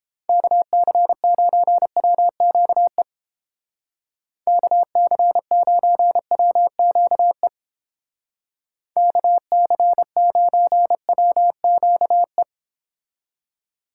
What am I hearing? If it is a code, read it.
Morse code, heard as a keyed tone, about 29 wpm, 700 Hz. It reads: KC9WQE KC9WQE KC9WQE